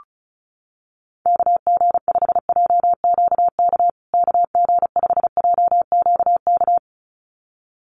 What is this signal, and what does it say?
Morse code, heard as a keyed tone, about 34 wpm, 700 Hz. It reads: KG5JQK KG5JQK
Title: USA Call Signs Repeated 2x 35wpm